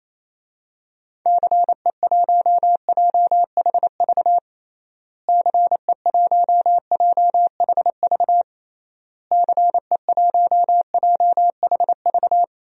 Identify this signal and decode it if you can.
Morse code, heard as a keyed tone, about 28 wpm, 700 Hz. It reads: CE1JHV CE1JHV CE1JHV
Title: DXCC-100 Call Signs - 6 Characters - Repeated 3x - 28wpm